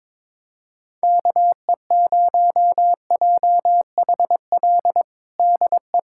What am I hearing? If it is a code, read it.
Morse code, heard as a keyed tone, about 22 wpm, 700 Hz. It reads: KE0JHL DE
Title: USA Callsign DE Callsign 22wpm